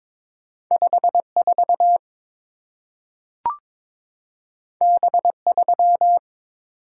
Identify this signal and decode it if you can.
Morse code, heard as a keyed tone, about 22 wpm, 690 Hz. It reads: 54 B3
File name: practice / 22wpm / Two Letter-Number Combinations - 22wpm